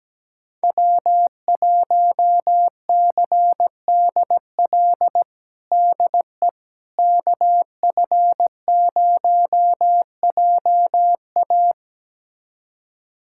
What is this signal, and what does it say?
Morse code, heard as a keyed tone, about 17 wpm, 700 Hz. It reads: W1CDL DE KF0JA
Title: USA Callsign DE Callsign - 5 Characters - 17wpm